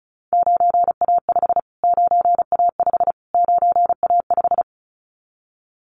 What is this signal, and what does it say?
Morse code, heard as a keyed tone, about 34 wpm, 700 Hz. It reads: 9A5 9A5 9A5
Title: Callsign Prefixes for Top 25 DXCC - Repeated 3x - 35wpm